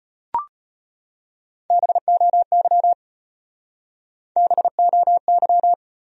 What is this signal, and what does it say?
Morse code, heard as a keyed tone, about 35 wpm, 700 Hz. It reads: BOY BOY